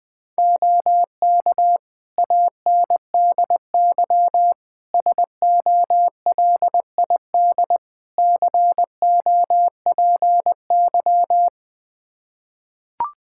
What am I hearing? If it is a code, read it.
Morse code, heard as a keyed tone, about 21 wpm, 700 Hz. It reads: OK ANDY SOLID COPY
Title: K1USN SST QSO Practice 20wpm